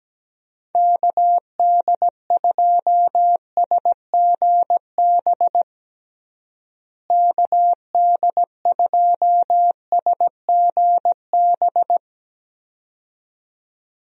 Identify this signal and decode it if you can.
Morse code, heard as a keyed tone, about 17 wpm, 700 Hz. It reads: KD2SGB KD2SGB
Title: USA Call Signs - 6 Characters - Repeated 2x - 17wpm